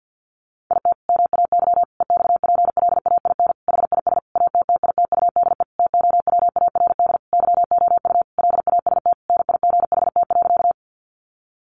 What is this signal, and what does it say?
Morse code, heard as a keyed tone, about 50 wpm, 700 Hz. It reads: IT MAY EXPLAIN HIS ATTITUDE TOWARD YOU LAST NIGHT.
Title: Sentences from Top 1300 Words 50wpm